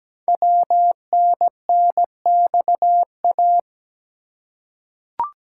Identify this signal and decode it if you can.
Morse code, heard as a keyed tone, about 17 wpm, 700 Hz. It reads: WNNXA
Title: Code Groups - 5 Letters - 17wpm